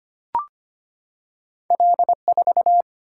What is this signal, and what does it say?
Morse code, heard as a keyed tone, about 25 wpm, 690 Hz. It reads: L4
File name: Two Letter-Number Combinations - 25wpm